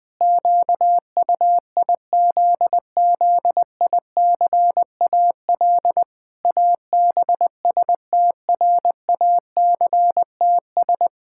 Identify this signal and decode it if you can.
Morse code, heard as a keyed tone, about 20 wpm, 700 Hz. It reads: QUIZZICAL ABSTRACTS
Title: Pangram Challenge v2 - 20wpm